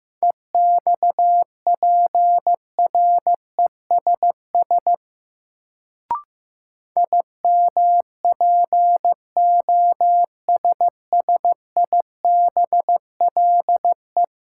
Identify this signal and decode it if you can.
Morse code, heard as a keyed tone, about 15 wpm, 700 Hz. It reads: EXPRESS IMPOSSIBLE
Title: Sentences from Top 1100 Words - New Word Drill - 15wpm